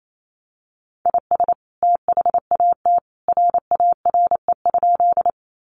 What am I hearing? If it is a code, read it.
Morse code, heard as a keyed tone, about 28 wpm, 700 Hz. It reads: IS THAT RARE?
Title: Sentences from Top 1100 Words 28wpm